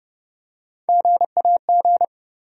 Morse code, heard as a keyed tone, about 30 wpm, 700 Hz. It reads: GAG
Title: USA Call Sign Postfixes - 3 Letters 30wpm